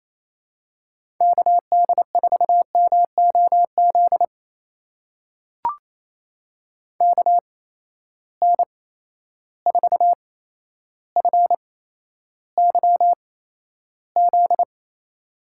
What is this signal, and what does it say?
Morse code, heard as a keyed tone, about 27 wpm, spaced as for 12 wpm, 700 Hz. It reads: KD4MOZ KN4FYZ
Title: USA Call Signs - 6 Characters - 8x Character Spacing - 28wpm